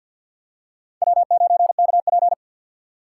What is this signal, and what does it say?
Morse code, heard as a keyed tone, about 52 wpm, 700 Hz. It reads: W9KP
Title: USA Call Signs - 4 Characters - 3x Character Spacing - 50wpm